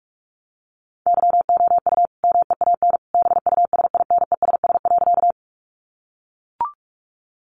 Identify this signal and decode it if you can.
Morse code, heard as a keyed tone, about 44 wpm, 700 Hz. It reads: YOU MEAN BUSINESS.
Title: Sentences from Top 600 Words 45wpm